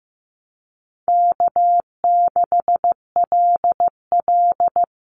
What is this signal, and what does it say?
Morse code, heard as a keyed tone, about 15 wpm, 700 Hz. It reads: K6LL
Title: Easy Call Signs - 15wpm